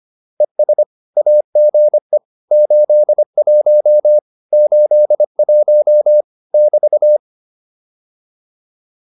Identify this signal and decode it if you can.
Morse code, heard as a keyed tone, about 25 wpm, 600 Hz. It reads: ES AGE 81 81 <BT>
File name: QSO401a 25